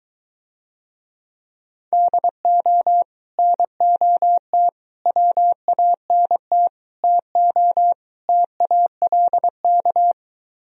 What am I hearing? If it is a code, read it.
Morse code, heard as a keyed tone, about 23 wpm, 700 Hz. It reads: DO NOT WANT TO TALK